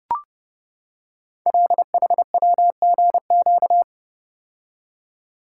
Morse code, heard as a keyed tone, about 30 wpm, 690 Hz. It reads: LHWGQ
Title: Code Groups - 5 Letters - 30wpm